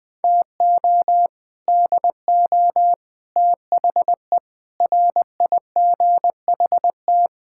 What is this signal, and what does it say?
Morse code, heard as a keyed tone, about 20 wpm, 700 Hz. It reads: TO DO THE RIGHT
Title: Build A Phrase - 5-Word Phrases - 20wpm